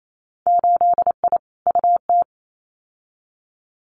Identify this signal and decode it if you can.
Morse code, heard as a keyed tone, about 28 wpm, 690 Hz. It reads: 8I UT
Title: Winter Field Day Exchanges Encore 28wpm